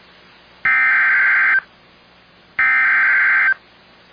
Emergency Alert System (SAME) tones (header bursts), heard as harsh buzzing bursts. It is heard from 0.7 to 3.5 s.